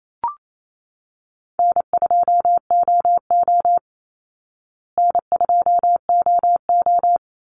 Morse code, heard as a keyed tone, about 28 wpm, 700 Hz. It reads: N2OO N2OO
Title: Call Signs Repeated 3x - 28wpm